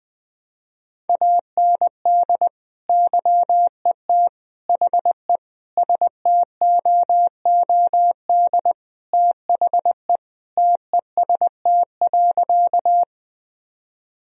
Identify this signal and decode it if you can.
Morse code, heard as a keyed tone, about 20 wpm, 700 Hz. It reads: AND YET HE STOOD THE TEST.
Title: Sentences from Top 1000 Words - Review Part 3 - 20wpm